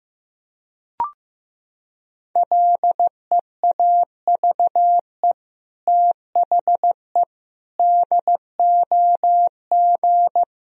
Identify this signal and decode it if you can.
Morse code, heard as a keyed tone, about 15 wpm, 700 Hz. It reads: LEAVE THE DOG
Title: Sentences from Top 1000 Words - Review Part 2 - 15wpm